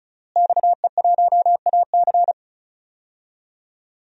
Morse code, heard as a keyed tone, about 34 wpm, 700 Hz. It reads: XE1AC
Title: DXCC-100 Call Signs - 5 Characters - Repeated 3x - 35wpm